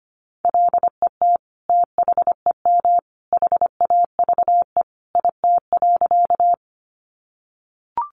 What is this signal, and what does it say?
Morse code, heard as a keyed tone, about 25 wpm, 700 Hz. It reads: LET THEM HAVE IT.
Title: Sentences from Top 300 Words 25wpm